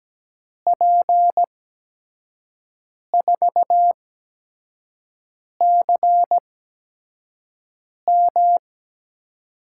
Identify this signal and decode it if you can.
Morse code, heard as a keyed tone, about 17 wpm, spaced as for 5 wpm, 700 Hz. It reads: P4CM
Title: USA Call Signs - 5 Characters - 8x Character Spacing - 17wpm